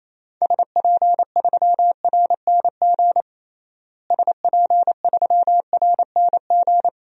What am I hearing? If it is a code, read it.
Morse code, heard as a keyed tone, about 28 wpm, 700 Hz. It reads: SP3RNG SP3RNG
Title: DXCC-100 Call Signs - 6 Characters - Repeated 3x - 28wpm